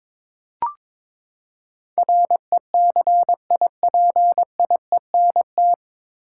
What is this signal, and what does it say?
Morse code, heard as a keyed tone, about 22 wpm, 700 Hz. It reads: RECIPIENT
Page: Morse Code Ninja practice 22wpm